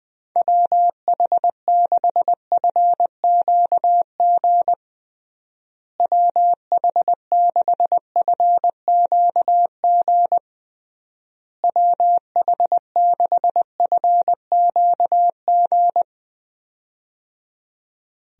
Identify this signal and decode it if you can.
Morse code, heard as a keyed tone, about 20 wpm, 700 Hz. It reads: WH6FQG WH6FQG WH6FQG
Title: USA Call Signs - 6 Characters - Repeated 3x - 20wpm